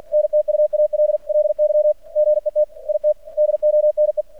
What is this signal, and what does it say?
Morse code, heard as a keyed tone, about 48 wpm, 600 Hz. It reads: NTUALLY GET IT DONE
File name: archived / CRSnet